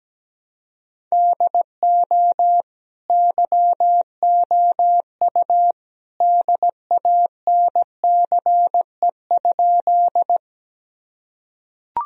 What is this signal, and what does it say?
Morse code, heard as a keyed tone, about 17 wpm, 700 Hz. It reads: DO YOU DANCE?
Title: Sentences from Top 1000 Words - Review Part 4 - 17wpm